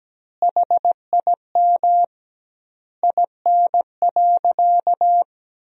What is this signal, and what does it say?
Morse code, heard as a keyed tone, about 17 wpm, 700 Hz. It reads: HIM IN.
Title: 2x Word-Spacing - Sentences from Top 100 Words - 17wpm